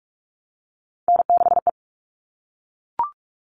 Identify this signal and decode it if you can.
Morse code, heard as a keyed tone, about 44 wpm, 700 Hz. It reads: N6E
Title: All Cut Numbers - 3-Digit 45wpm